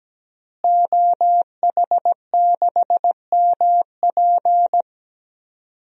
Morse code, heard as a keyed tone, about 17 wpm, 700 Hz. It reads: OH6MP
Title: DXCC-100 Call Signs - 5 Characters - Repeated 3x - 17wpm